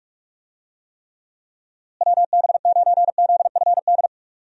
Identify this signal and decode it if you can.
Morse code, heard as a keyed tone, about 45 wpm, 700 Hz. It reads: WB9ZFD
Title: USA Call Signs - 6 Characters - Repeated 2x - 45wpm